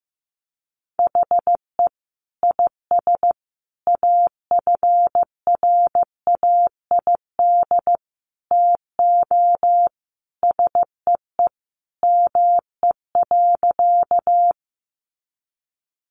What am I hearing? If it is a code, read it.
Morse code, heard as a keyed tone, about 15 wpm, 700 Hz. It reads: HE IS AFRAID TO SEE ME.
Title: Sentences from Top 1500 Words 15wpm